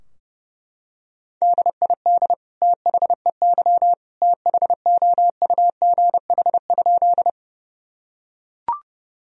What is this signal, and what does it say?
Morse code, heard as a keyed tone, about 30 wpm, 700 Hz. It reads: DID THEY THOUGH?
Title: Sentences from Top 1000 Words - Review Part 2 - 30wpm